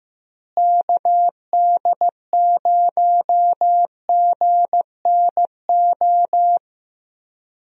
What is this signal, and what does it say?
Morse code, heard as a keyed tone, about 15 wpm, 700 Hz. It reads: KD0GNO
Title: USA Call Signs - 6 Characters - Repeated 2x - 15wpm